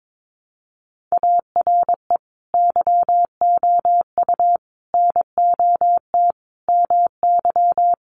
Morse code, heard as a keyed tone, about 22 wpm, 700 Hz. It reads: ARE YOU NOT MY
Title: Sentences from Top 1300 Words 22wpm